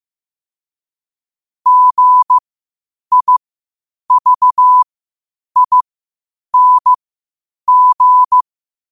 Morse code, heard as a keyed tone, about 15 wpm, spaced as for 8 wpm, 1000 Hz. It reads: GIVING
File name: qotdfiles / 8wpm